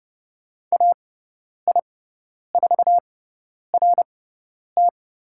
Morse code, heard as a keyed tone, about 30 wpm, spaced as for 10 wpm, 700 Hz. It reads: AI4RT
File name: Lesson 17g - Letter u - Call Signs 30wpm